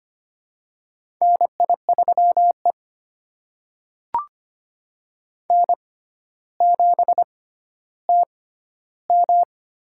Morse code, heard as a keyed tone, about 25 wpm, spaced as for 13 wpm, 700 Hz. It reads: NI3E N7TM
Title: USA Call Signs - 4 Characters - 6x Character Spacing - 25wpm